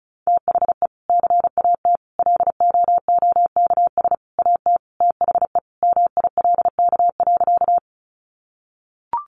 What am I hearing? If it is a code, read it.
Morse code, heard as a keyed tone, about 34 wpm, 700 Hz. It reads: THE CAT LOOKS AT THE MILK.
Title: Additional Easy Phrases 35wpm